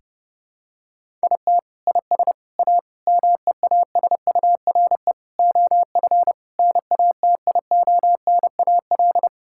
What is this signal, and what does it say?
Morse code, heard as a keyed tone, about 30 wpm, 700 Hz. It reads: IT IS A MEASURE OF NATIONAL
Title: Sentences from Top 1600 Words 30wpm